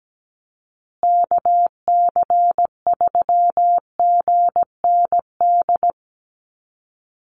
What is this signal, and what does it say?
Morse code, heard as a keyed tone, about 17 wpm, 700 Hz. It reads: KC3GND
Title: USA Call Signs Repeated 2x 17wpm